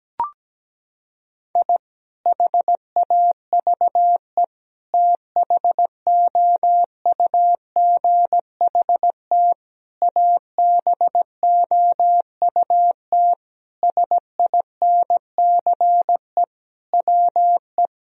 Morse code, heard as a keyed tone, about 17 wpm, 700 Hz. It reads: I HAVE THOUGHT ABOUT SINCE WE
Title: Sentences from Top 1000 Words - Review Part 1 - 17wpm